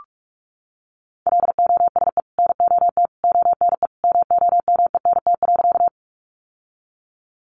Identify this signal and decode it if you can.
Morse code, heard as a keyed tone, about 44 wpm, 700 Hz. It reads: LOSE NOT ONE MOMENT.
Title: Sentences from Top 1100 Words 45wpm